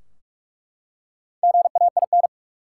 Morse code, heard as a keyed tone, about 42 wpm, 700 Hz. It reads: GAIN
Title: Four Letter Words - 45wpm